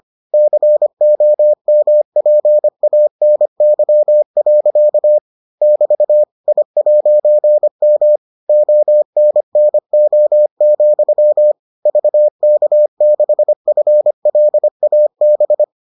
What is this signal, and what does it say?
Morse code, heard as a keyed tone, about 25 wpm, 600 Hz. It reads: COMPANY. <BT> I'M ONNO, VK6FLAB